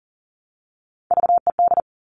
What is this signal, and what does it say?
Morse code, heard as a keyed tone, about 42 wpm, 690 Hz. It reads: VED
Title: Top Three Letter Combinations in Words 40wpm